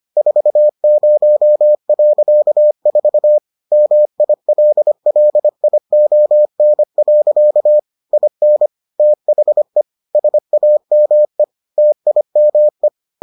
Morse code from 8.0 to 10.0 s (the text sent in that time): IN THE